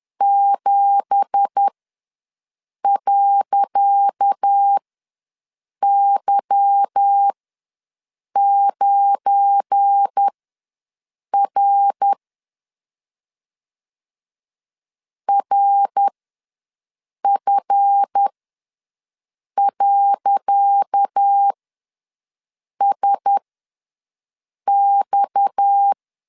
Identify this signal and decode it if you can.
Morse code, heard as a keyed tone, about 10 wpm, spaced as for 5 wpm, 800 Hz. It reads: 7.Y9R RF.SX
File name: LEZIONI / 10 SIGNS / 10C DOT ws 10wpm 4213